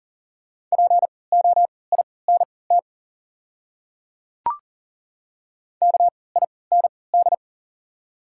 Morse code, heard as a keyed tone, about 39 wpm, spaced as for 22 wpm, 700 Hz. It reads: POINT KIND
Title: 3x Character Spacing - Top 300 Words - 40wpm